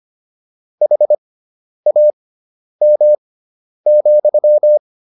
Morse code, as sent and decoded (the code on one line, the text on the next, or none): .... .- -- --..--
HAM,